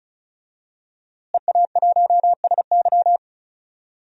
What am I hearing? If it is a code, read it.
Morse code, heard as a keyed tone, about 36 wpm, 700 Hz. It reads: EA1SY
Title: DXCC-100 Call Signs - 5 Characters - Repeated 3x - 35wpm